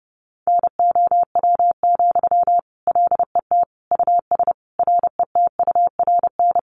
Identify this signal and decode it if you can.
Morse code, heard as a keyed tone, about 30 wpm, 700 Hz. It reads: NOW, LET US RETURN
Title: Sentences from Top 800 Words 30wpm